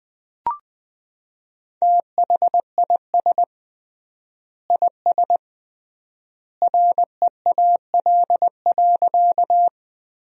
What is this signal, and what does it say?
Morse code, heard as a keyed tone, about 21 wpm, 700 Hz. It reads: THIS IS REAL.
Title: 3x Word-Spacing - Sentences from Top 300 Words - 20wpm